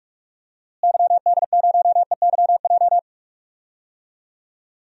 Morse code, heard as a keyed tone, about 46 wpm, 690 Hz. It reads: YD0EYJ
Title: DXCC-100 Call Signs - 6 Characters - Repeated 2x - 45wpm